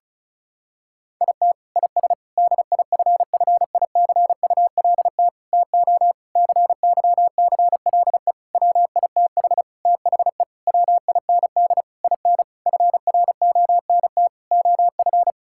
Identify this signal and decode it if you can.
Morse code, heard as a keyed tone, about 36 wpm, 700 Hz. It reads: IT IS DIFFICULT TO CYCLE WITH THE WIND IN FRONT OF
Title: Sentences from Top 1000 Words - Review Part 6 - 35wpm